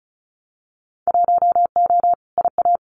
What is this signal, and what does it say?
Morse code, heard as a keyed tone, about 36 wpm, 700 Hz. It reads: 1O IA
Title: Winter Field Day Exchanges Encore 35wpm